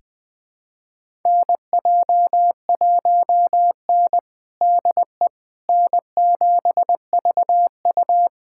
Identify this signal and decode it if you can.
Morse code, heard as a keyed tone, about 20 wpm, 700 Hz. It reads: NJ1N DE N7VU
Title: USA Callsign DE Callsign - 4 Characters - 20wpm